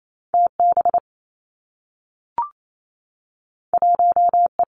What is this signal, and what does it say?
Morse code, heard as a keyed tone, about 28 wpm, 700 Hz. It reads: T B 1 E
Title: Two Challenging Letter-Number Combinations 28wpm